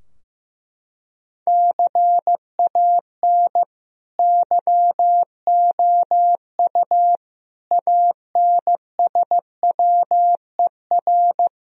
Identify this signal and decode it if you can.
Morse code, heard as a keyed tone, about 15 wpm, 700 Hz. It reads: CAN YOU ANSWER
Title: Sentences from Top 1000 Words - Review Part 3 - 15wpm